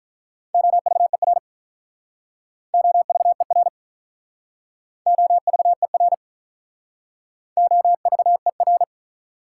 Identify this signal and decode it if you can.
Morse code, heard as a keyed tone, about 47 wpm, 700 Hz. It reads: OVER OVER OVER OVER